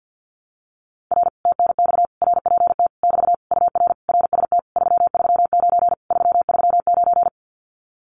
Morse code, heard as a keyed tone, about 50 wpm, 700 Hz. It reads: R TNX RPT <BT> UR RST 339 339
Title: Common QSO Phrases Encore 50wpm